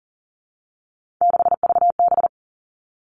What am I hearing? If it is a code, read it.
Morse code, heard as a keyed tone, about 38 wpm, 700 Hz. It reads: 6VB
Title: Three Challenging Letter-Number Combinations 40wpm